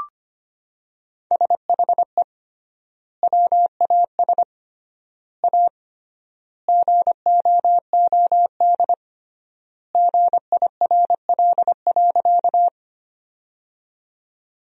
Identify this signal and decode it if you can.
Morse code, heard as a keyed tone, about 25 wpm, 700 Hz. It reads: SHE WAS A GOOD GIRL.
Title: 3x Word-Spacing - Sentences from Top 400 Words - 25wpm